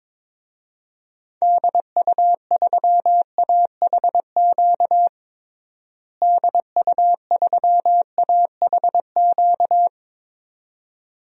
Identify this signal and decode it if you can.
Morse code, heard as a keyed tone, about 22 wpm, 700 Hz. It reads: DU3AHQ DU3AHQ
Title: DXCC-100 Call Signs - 6 Characters - Repeated 2x - 22wpm